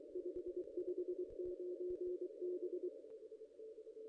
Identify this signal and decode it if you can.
Morse code, heard as a keyed tone, about 23 wpm, 360 Hz. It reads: 559 B
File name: contests / ss-cw-k5tr-op-n5rz -1500